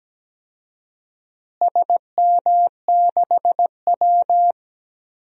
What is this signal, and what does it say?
Morse code, heard as a keyed tone, about 17 wpm, 700 Hz. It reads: SM6W
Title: DXCC-100 Call Signs - 4 Characters - Repeated 3x - 17wpm